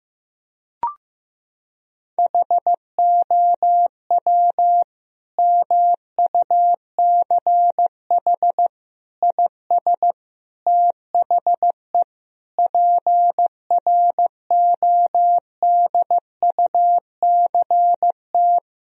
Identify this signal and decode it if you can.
Morse code, heard as a keyed tone, about 15 wpm, 700 Hz. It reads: HOW MUCH IS THE PRODUCT
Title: Sentences from Top 1000 Words - Review Part 2 - 15wpm